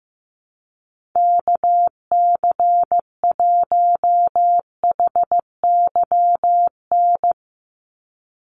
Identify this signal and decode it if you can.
Morse code, heard as a keyed tone, about 15 wpm, 700 Hz. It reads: KC1HYN
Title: USA Call Signs 15wpm